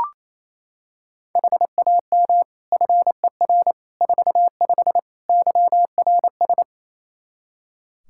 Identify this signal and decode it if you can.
Morse code, heard as a keyed tone, about 27 wpm, 700 Hz. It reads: HAM FER 45 YRS
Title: Practice QSO - Exchange 3 - More Info - 28wpm